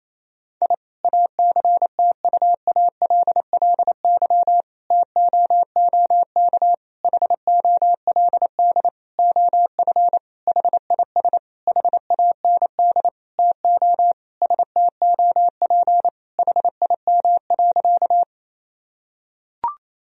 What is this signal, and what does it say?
Morse code, heard as a keyed tone, about 28 wpm, 700 Hz. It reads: I ACTUALLY TOOK HOLD OF HIS HAND TO STOP HIM.
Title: Sentences from Top 1000 Words - Review Part 4 - 28wpm